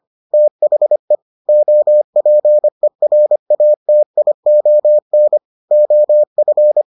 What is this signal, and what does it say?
Morse code, heard as a keyed tone, about 25 wpm, 600 Hz. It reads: THE OPERATION OF